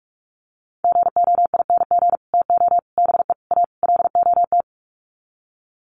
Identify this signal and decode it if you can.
Morse code, heard as a keyed tone, about 44 wpm, 700 Hz. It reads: GOING TO BE A LOT
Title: Sets of 5 Words Encore 45wpm